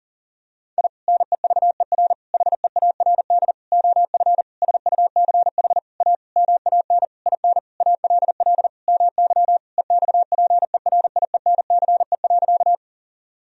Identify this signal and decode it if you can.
Morse code, heard as a keyed tone, about 40 wpm, 700 Hz. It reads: I NEVER HEARD OF SUCH A MAN IN ALL MY EXPERIENCE.
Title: Sentences from Top 1000 Words - Review Part 3 - 40wpm